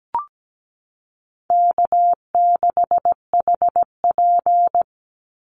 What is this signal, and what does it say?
Morse code, heard as a keyed tone, about 17 wpm, 700 Hz. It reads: K6HP
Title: Easy Call Signs - 17wpm